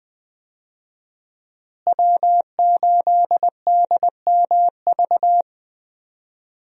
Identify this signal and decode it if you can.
Morse code, heard as a keyed tone, about 20 wpm, 700 Hz. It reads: W8DMV
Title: USA Call Signs - 5 Characters - Repeated 2x - 20wpm